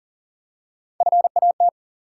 Morse code, heard as a keyed tone, about 42 wpm, 700 Hz. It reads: FAT
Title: Three Letter Words - 40wpm